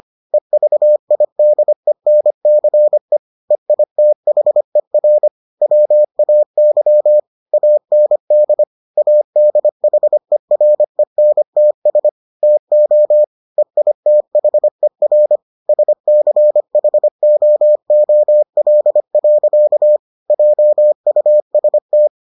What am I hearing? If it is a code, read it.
Morse code, heard as a keyed tone, about 25 wpm, 600 Hz. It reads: EVIDENCE EITHER WAY AND ADHERENTS TO EITHER SCHOOL. JUST